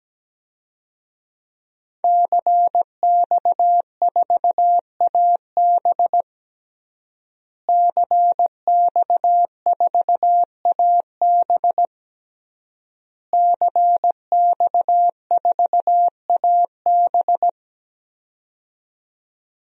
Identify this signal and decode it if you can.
Morse code, heard as a keyed tone, about 17 wpm, 700 Hz. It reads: CX4AB CX4AB CX4AB
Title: DXCC-100 Call Signs - 5 Characters - Repeated 3x - 17wpm